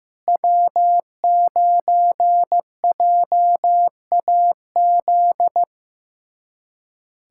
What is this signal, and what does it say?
Morse code, heard as a keyed tone, about 15 wpm, 700 Hz. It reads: W9JAZ
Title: USA Callsign DE Callsign - 5 Characters - 15wpm